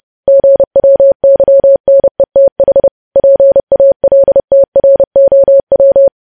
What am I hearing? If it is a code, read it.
Morse code, heard as a keyed tone, about 30 wpm, 550 Hz. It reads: GWYNETH PALTROW
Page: news headlines in morse code at 30 wpm